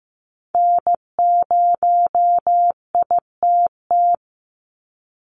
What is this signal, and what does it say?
Morse code, heard as a keyed tone, about 15 wpm, 700 Hz. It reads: N0ITT
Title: USA Call Signs 15wpm